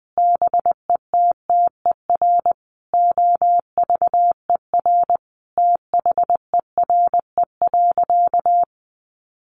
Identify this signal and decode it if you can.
Morse code, heard as a keyed tone, about 20 wpm, 700 Hz. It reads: BETTER OVER THERE.
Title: Sentences from Top 1500 Words 20wpm